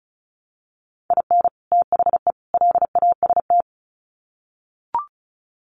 Morse code, heard as a keyed tone, about 34 wpm, 700 Hz. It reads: IN THE LAST
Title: Sets of 3 Words 35wpm